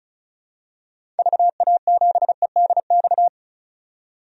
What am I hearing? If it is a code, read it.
Morse code, heard as a keyed tone, about 36 wpm, 700 Hz. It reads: VA7EDX
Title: DXCC-100 Call Signs - 6 Characters - 35wpm